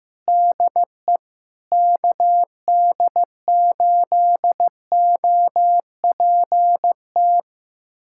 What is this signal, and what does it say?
Morse code, heard as a keyed tone, about 16 wpm, 700 Hz. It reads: DE KD8OPT
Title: USA Callsign DE Callsign 15wpm